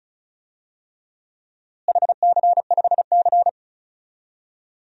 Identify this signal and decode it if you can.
Morse code, heard as a keyed tone, about 36 wpm, 700 Hz. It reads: HC5C
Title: DXCC-100 Call Signs - 4 Characters - Repeated 3x - 35wpm